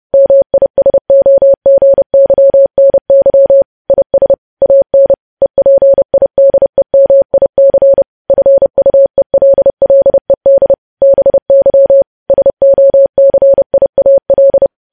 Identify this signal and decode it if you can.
Morse code, heard as a keyed tone, about 30 wpm, 550 Hz. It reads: MISOGYNY IS AN EPIDEMIC FUELLED BY SOCIAL
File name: podcast / News headlines-1x-30x30